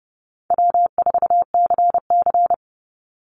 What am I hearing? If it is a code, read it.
Morse code, heard as a keyed tone, about 30 wpm, 700 Hz. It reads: W4CC
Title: Easy Call Signs Repeated 2x - 30wpm